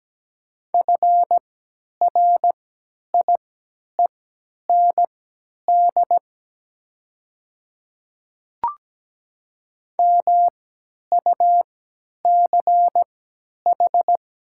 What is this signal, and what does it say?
Morse code, heard as a keyed tone, about 17 wpm, spaced as for 10 wpm, 700 Hz. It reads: FRIEND MUCH
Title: 3x Character Spacing - Top 500 Words - 17wpm